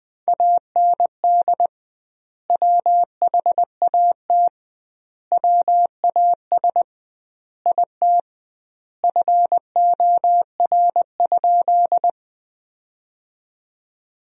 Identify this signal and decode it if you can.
Morse code, heard as a keyed tone, about 20 wpm, 700 Hz. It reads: AND WHAT WAS IT FOR?
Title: 2x Word-Spacing - Sentences from Top 100 Words - 20wpm